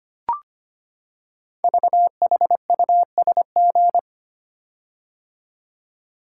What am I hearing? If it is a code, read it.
Morse code, heard as a keyed tone, about 25 wpm, 700 Hz. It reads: VHUSG
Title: Code Groups - 5 Letters - 25wpm